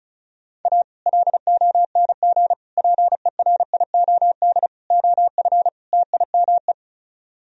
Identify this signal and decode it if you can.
Morse code, heard as a keyed tone, about 44 wpm, 700 Hz. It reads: A LONG PERIOD OF TIME
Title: Speed-Racing - Sets of 5 Words 35wpm